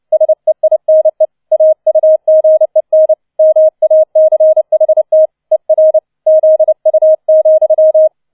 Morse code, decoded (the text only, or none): SEINE AUGEN MACHT ER ZU,